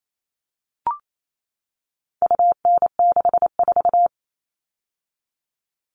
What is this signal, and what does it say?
Morse code, heard as a keyed tone, about 28 wpm, 700 Hz. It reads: UN64
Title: All Cut Numbers - 4-Digit 28wpm